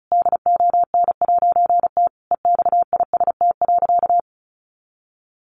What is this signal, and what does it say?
Morse code, heard as a keyed tone, about 34 wpm, 700 Hz. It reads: DON'T EXIST.
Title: A Sound Of Thunder 35wpm